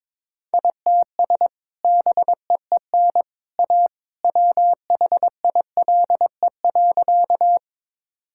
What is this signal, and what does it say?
Morse code, heard as a keyed tone, about 22 wpm, 700 Hz. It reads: ITS BEEN A WHILE.
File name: practice / 22wpm / Sentences with 2-Word Phrases - 22wpm